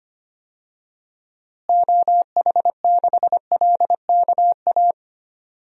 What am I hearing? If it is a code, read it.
Morse code, heard as a keyed tone, about 25 wpm, 700 Hz. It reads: OH6LKA
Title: DXCC-100 Call Signs - 6 Characters - Repeated 3x - 25wpm